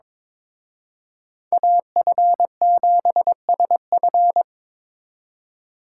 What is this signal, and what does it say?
Morse code, heard as a keyed tone, about 26 wpm, 700 Hz. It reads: AF7SF
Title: Speed Racing - USA 2x2 Call Signs 22wpm